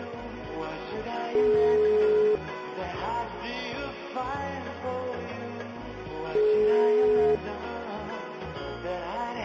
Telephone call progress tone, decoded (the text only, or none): ringback tone